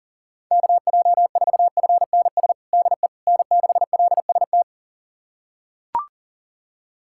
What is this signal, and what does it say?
Morse code, heard as a keyed tone, about 40 wpm, 700 Hz. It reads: KJ4FNS DE N6LST
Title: USA Callsign DE Callsign 40wpm